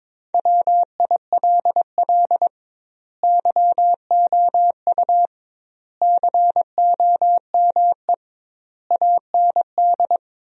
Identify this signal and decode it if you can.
Morse code, heard as a keyed tone, about 22 wpm, 700 Hz. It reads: WILL YOU COME AND
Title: 2x Word-Spacing - Sentences from Top 100 Words - 22wpm